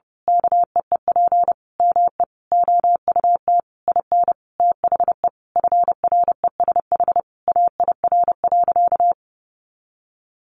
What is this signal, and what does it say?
Morse code, heard as a keyed tone, about 30 wpm, 700 Hz. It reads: KEEP ME OUT IN THE FRESH AIR.
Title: Sentences from Top 1200 Words 30wpm